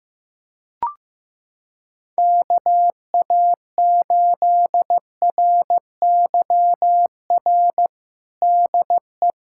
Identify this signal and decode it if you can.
Morse code, heard as a keyed tone, about 15 wpm, 700 Hz. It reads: KA8RYR DE
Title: USA Callsign DE Callsign 15wpm